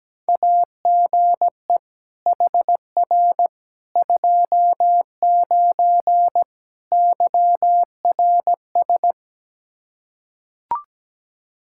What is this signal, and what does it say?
Morse code, heard as a keyed tone, about 17 wpm, 700 Hz. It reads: AGE HR 29 YRS
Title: Practice QSO - Exchange 3 - More Info - 17wpm